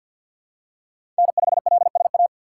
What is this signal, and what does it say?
Morse code, heard as a keyed tone, about 53 wpm, 700 Hz. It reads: N5LSA
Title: USA Call Signs - 5 Characters - 6x Character Spacing - 50wpm